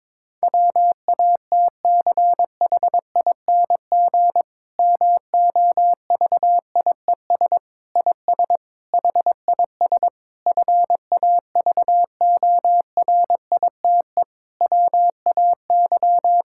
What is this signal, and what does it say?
Morse code, heard as a keyed tone, about 22 wpm, 700 Hz. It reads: WATCHING MOVIES IS HIS FAVORITE WAY
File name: Difficult Sentences with Hints - 22wpm